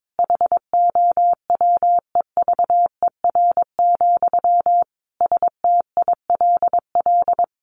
Morse code, heard as a keyed tone, about 22 wpm, 700 Hz. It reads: HOWEVER, STILL
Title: Sentences from Top 1400 Words 22wpm